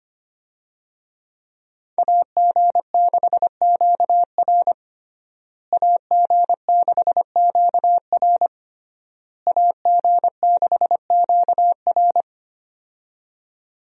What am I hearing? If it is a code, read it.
Morse code, heard as a keyed tone, about 25 wpm, 700 Hz. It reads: AG6QR AG6QR AG6QR
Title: USA Call Signs - 5 Characters - Repeated 3x - 25wpm